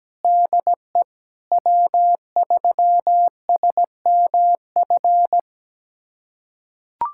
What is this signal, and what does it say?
Morse code, heard as a keyed tone, about 17 wpm, 700 Hz. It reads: DE W3SMF
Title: USA Callsign DE Callsign - 5 Characters - 17wpm